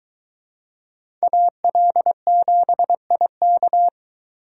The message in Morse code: AL7IK

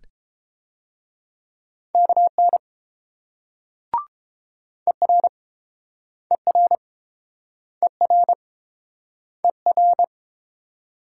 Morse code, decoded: KN ER ER ER ER

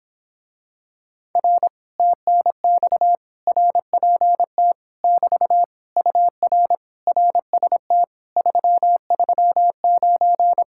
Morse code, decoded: R TNX RPT <BT> UR RST 339

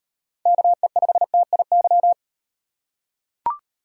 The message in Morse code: KE5TIY

